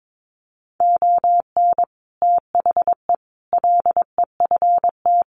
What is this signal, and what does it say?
Morse code, heard as a keyed tone, about 22 wpm, 700 Hz. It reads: ON THE LEFT
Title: Sets of 5 Words Encore2 22wpm